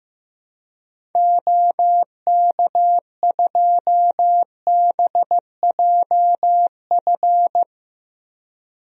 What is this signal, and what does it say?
Morse code, heard as a keyed tone, about 15 wpm, 700 Hz. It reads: OK2BJF
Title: DXCC-100 Call Signs - 6 Characters - Repeated 3x - 15wpm